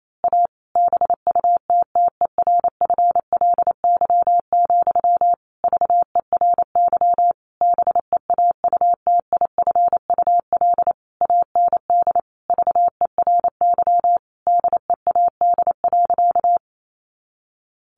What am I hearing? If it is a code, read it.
Morse code, heard as a keyed tone, about 28 wpm, 700 Hz. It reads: A BUTTERFLY, VERY BEAUTIFUL AND VERY DEAD.
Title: A Sound Of Thunder 28wpm